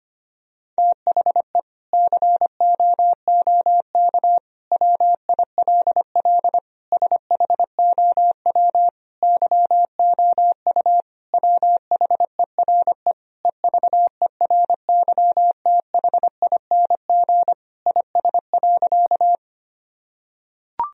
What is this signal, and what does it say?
Morse code, heard as a keyed tone, about 25 wpm, 700 Hz. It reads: THE COOK WILL SHOW YOU WHERE EVERYTHING IS.
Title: Sentences from Top 1000 Words - Review Part 4 - 25wpm